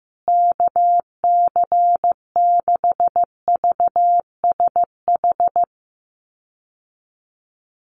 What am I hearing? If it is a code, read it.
Morse code, heard as a keyed tone, about 15 wpm, 700 Hz. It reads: KC6VSH
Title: USA Call Signs Repeated 2x 15wpm